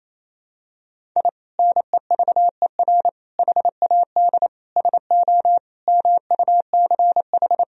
Morse code, heard as a keyed tone, about 28 wpm, 700 Hz. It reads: I NEVER HAD SO MUCH
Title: Sentences from Top 1000 Words - Review Part 4 - 28wpm